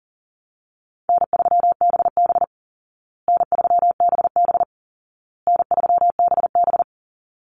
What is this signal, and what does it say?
Morse code, heard as a keyed tone, about 40 wpm, 700 Hz. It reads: N3BB N3BB N3BB
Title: Easy Call Signs Repeated 3x - 40wpm